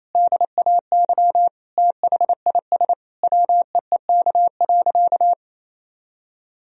Morse code, heard as a keyed tone, about 28 wpm, 700 Hz. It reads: DAY THIS WEEK.
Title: Sentences from Top 1000 Words - Review Part 2 - 28wpm